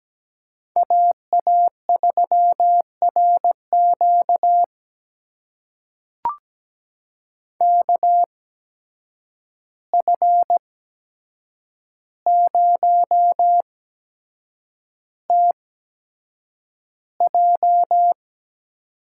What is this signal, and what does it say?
Morse code, heard as a keyed tone, about 17 wpm, spaced as for 7 wpm, 700 Hz. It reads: AA3RQ KF0TJ
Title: USA Call Signs - 5 Characters - 8x Character Spacing - 17wpm